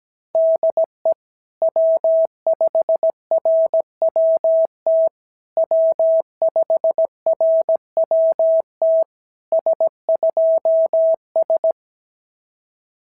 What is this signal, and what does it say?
Morse code, heard as a keyed tone, about 17 wpm, 660 Hz. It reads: DE W5RWT W5RWT S2S
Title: SOTA QSO Practice 17wpm